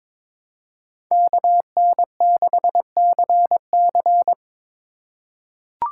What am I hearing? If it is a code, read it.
Morse code, heard as a keyed tone, about 22 wpm, 700 Hz. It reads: KN6CC
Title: USA Call Signs - 5 Characters - 3x Character Spacing - 22wpm